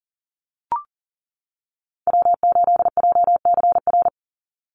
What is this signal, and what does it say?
Morse code, heard as a keyed tone, about 40 wpm, 700 Hz. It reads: W8JCR
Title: USA Call Signs 40wpm